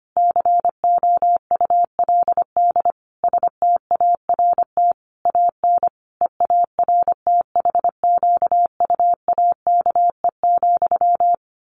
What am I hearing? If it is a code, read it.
Morse code, heard as a keyed tone, about 25 wpm, 700 Hz. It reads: COULD START AN EARTHQUAKE,
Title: A Sound Of Thunder 25wpm